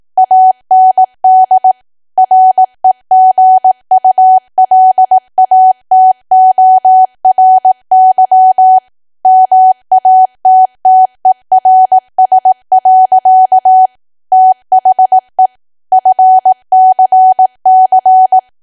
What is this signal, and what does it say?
Morse code, heard as a keyed tone, about 18 wpm, 750 Hz. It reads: AND REGULATORY MATTERS. THE FCC
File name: cwa / W1AW-Level2 / 18WPM